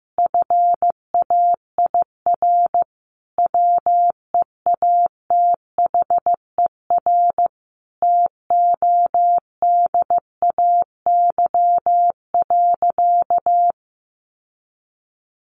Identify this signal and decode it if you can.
Morse code, heard as a keyed tone, about 15 wpm, 700 Hz. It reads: FAIR WEATHER TODAY.
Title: Sentences from Top 1300 Words 15wpm